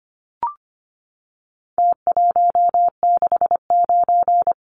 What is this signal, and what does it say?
Morse code, heard as a keyed tone, about 25 wpm, 700 Hz. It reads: T169
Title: Numbers - Contest Style - 4-Digit 25wpm